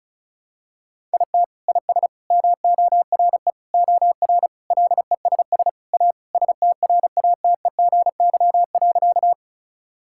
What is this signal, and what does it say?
Morse code, heard as a keyed tone, about 34 wpm, 700 Hz. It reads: IT IS MORE OR LESS A STRATEGY.
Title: Sentences from Top 1000 Words - Review Part 5 - 35wpm